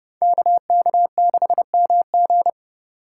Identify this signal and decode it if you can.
Morse code, heard as a keyed tone, about 30 wpm, 700 Hz. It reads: KK6MG
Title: USA Call Signs - 5 Characters - 3x Character Spacing - 30wpm